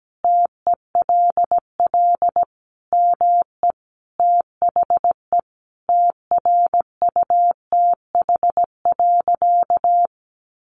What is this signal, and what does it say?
Morse code, heard as a keyed tone, about 17 wpm, 700 Hz. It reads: TELL ME THE TRUTH.
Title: Sentences from Top 1500 Words 17wpm